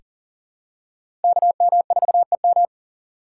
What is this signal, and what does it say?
Morse code, heard as a keyed tone, about 40 wpm, 700 Hz. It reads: KM4EM
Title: USA Call Signs - 5 Characters - 6x Character Spacing - 40wpm